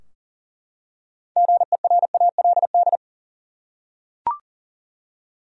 Morse code, heard as a keyed tone, about 40 wpm, 700 Hz. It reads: GERALD
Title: USA Common Names - 40wpm